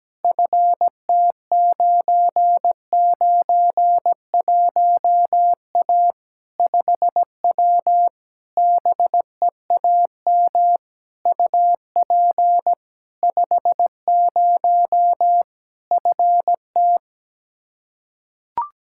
Morse code, heard as a keyed tone, about 17 wpm, 700 Hz. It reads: FT991A 5W BEAM UP 50 FT
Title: Practice QSO - Exchange 2 - Info - 17wpm